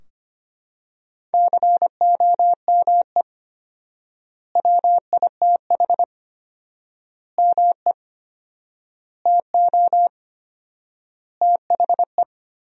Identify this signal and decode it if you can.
Morse code, heard as a keyed tone, about 25 wpm, 700 Hz. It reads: COME WITH ME TO THE
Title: 4x Word-Spacing - Sentences from Top 200 Words - 25wpm